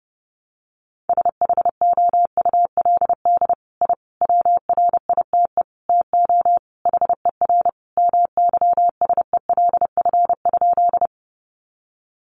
Morse code, heard as a keyed tone, about 30 wpm, 700 Hz. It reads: SHOULD I WRITE TO HER MYSELF?
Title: Sentences from Top 1300 Words 30wpm